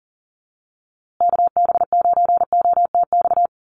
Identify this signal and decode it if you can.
Morse code, heard as a keyed tone, about 40 wpm, 700 Hz. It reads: KB9OTX
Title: USA Call Signs Repeated 2x 40wpm